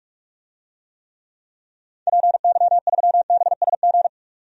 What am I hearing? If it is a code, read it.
Morse code, heard as a keyed tone, about 46 wpm, 700 Hz. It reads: PY3BSG
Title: DXCC-100 Call Signs - 6 Characters - 45wpm